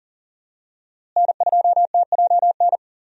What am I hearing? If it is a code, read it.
Morse code, heard as a keyed tone, about 39 wpm, 700 Hz. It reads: N2TJN
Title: USA Call Signs - 5 Characters - Repeated 3x - 40wpm